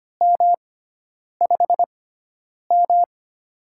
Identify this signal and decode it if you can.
Morse code, heard as a keyed tone, about 25 wpm, spaced as for 9 wpm, 700 Hz. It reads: M5M
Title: USA Call Signs - 6x Character Spacing - 25wpm